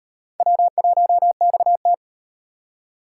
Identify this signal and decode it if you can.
Morse code, heard as a keyed tone, about 38 wpm, 700 Hz. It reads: W1XT